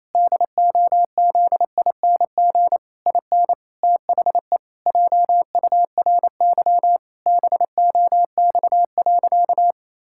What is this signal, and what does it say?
Morse code, heard as a keyed tone, about 28 wpm, 700 Hz. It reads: DOZING IN THE JURY BOX.
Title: Pangram Challenge v2 - 28wpm